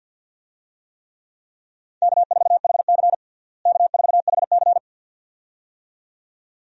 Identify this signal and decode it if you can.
Morse code, heard as a keyed tone, about 53 wpm, 700 Hz. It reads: K4HC K4HC
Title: USA Call Signs - 4 Characters - Repeated 2x - 50wpm